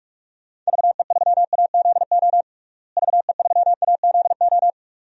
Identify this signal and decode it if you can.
Morse code, heard as a keyed tone, about 46 wpm, 700 Hz. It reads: VE3AZO VE3AZO
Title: DXCC-100 Call Signs - 6 Characters - Repeated 2x - 45wpm